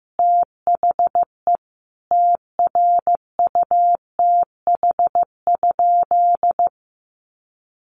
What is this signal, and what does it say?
Morse code, heard as a keyed tone, about 15 wpm, 700 Hz. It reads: THE TRUTH?
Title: Sentences from Top 1100 Words 15wpm